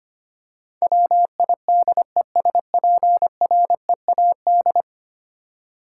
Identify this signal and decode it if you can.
Morse code, heard as a keyed tone, about 25 wpm, 700 Hz. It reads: WIDESPREAD
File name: Ten Letter Words - 25wpm